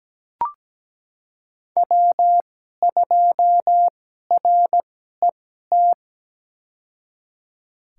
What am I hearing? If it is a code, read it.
Morse code, heard as a keyed tone, about 17 wpm, spaced as for 12 wpm, 700 Hz. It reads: W2RET
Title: USA Call Signs - 5 Characters - 2x Character Spacing - 17wpm